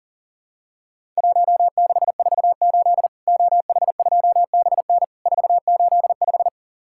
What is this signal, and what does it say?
Morse code, heard as a keyed tone, about 40 wpm, 700 Hz. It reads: 1648 OH2BN 485
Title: Worked All Europe DX Contest - QTC Exchange - 40wpm